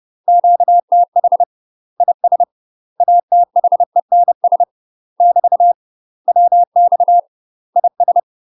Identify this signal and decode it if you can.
Morse code, heard as a keyed tone, about 30 wpm, 700 Hz. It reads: QTH IS ATHENS <BT> WX IS
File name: UK MASTER QSOs TRAINING / 30WPM / UK MASTER QSOs 0005